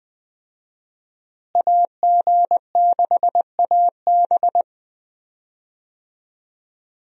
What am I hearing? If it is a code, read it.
Morse code, heard as a keyed tone, about 21 wpm, 700 Hz. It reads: AG6AB
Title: USA Call Signs - 5 Characters - Repeated 3x - 20wpm